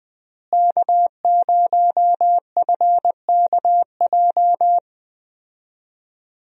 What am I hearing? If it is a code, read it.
Morse code, heard as a keyed tone, about 20 wpm, 700 Hz. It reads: K0FKJ
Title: USA Call Signs - 5 Characters - Repeated 2x - 20wpm